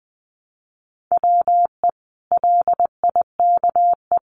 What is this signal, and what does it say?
Morse code, heard as a keyed tone, about 20 wpm, 700 Hz. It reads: WE LIKE
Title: Sentences from Top 1100 Words 20wpm